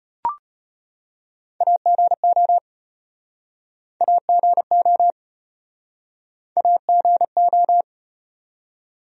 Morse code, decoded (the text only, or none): AGO AGO AGO